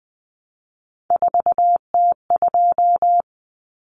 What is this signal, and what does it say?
Morse code, heard as a keyed tone, about 20 wpm, 700 Hz. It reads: 4T2
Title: Common Cut Numbers - 3-Digit 20wpm